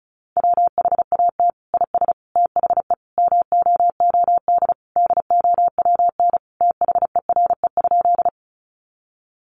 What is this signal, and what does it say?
Morse code, heard as a keyed tone, about 34 wpm, 700 Hz. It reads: WHAT IS THE MOOD DOWN THERE?
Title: Sentences from Top 1100 Words 35wpm